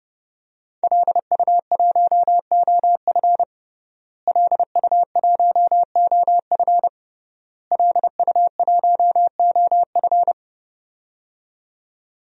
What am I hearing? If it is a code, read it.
Morse code, heard as a keyed tone, about 30 wpm, 700 Hz. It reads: LU1OF LU1OF LU1OF
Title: DXCC-100 Call Signs - 5 Characters - Repeated 3x - 30wpm